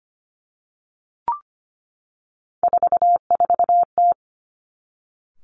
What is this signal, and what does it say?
Morse code, heard as a keyed tone, about 25 wpm, 700 Hz. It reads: 44T
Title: Three Challenging Letter-Number Combinations 25wpm